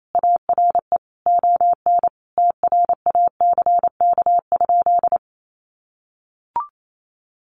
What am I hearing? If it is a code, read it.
Morse code, heard as a keyed tone, about 28 wpm, 700 Hz. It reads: ARE ON TRACK?
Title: Sentences from Top 1200 Words 28wpm